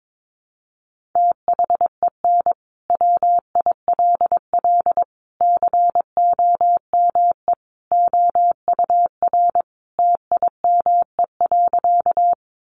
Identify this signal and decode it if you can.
Morse code, heard as a keyed tone, about 22 wpm, 700 Hz. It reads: THEN WILL COME OUR TIME.
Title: Sentences from Top 200 Words 22wpm